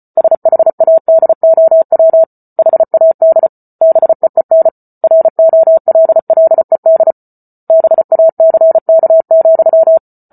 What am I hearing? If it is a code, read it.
Morse code, heard as a keyed tone, about 34 wpm, 650 Hz. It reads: SHADOW HAD BEEN ROLLED BACK,